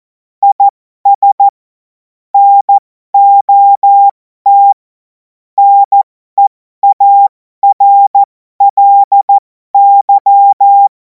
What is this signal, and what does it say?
Morse code, heard as a keyed tone, about 14 wpm, spaced as for 12 wpm, 800 Hz. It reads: IS NOT NEARLY
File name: Morsekurs / Witze